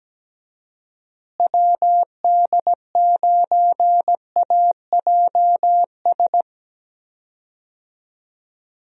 Morse code, heard as a keyed tone, about 17 wpm, 700 Hz. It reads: WD9AJS
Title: USA Call Signs - 6 Characters - Repeated 3x - 17wpm